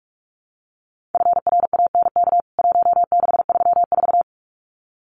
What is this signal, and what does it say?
Morse code, heard as a keyed tone, about 44 wpm, 700 Hz. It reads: FRANK 1634
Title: CWT Exchange 45wpm